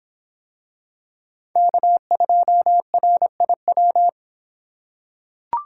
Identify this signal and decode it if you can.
Morse code, heard as a keyed tone, about 25 wpm, 700 Hz. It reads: K2RIW